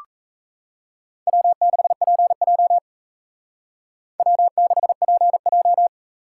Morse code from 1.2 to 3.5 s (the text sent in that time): W6PJ